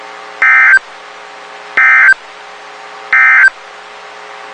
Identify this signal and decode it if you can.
Emergency Alert System (SAME) tones, heard as harsh buzzing bursts. It is heard from 0.3 to 3.6 s.
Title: WXK91 NOAA Weather Radio